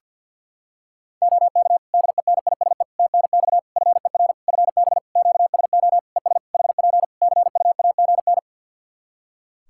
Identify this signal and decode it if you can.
Morse code, heard as a keyed tone, about 50 wpm, 700 Hz. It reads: OK DENISE TNX FER FB QSO ES HP CUAGN
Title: Practice QSO - Exchange 4 - Ending - 50wpm